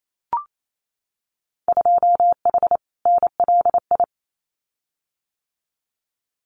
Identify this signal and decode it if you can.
Morse code, heard as a keyed tone, about 28 wpm, 700 Hz. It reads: 2H NLI
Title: Winter Field Day Exchanges Encore 28wpm